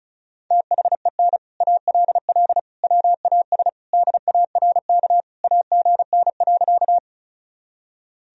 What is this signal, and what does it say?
Morse code, heard as a keyed tone, about 36 wpm, 700 Hz. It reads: THEN ALL WAS DARK AGN.
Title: Sentences from Top 500 Words and Abbreviations 35wpm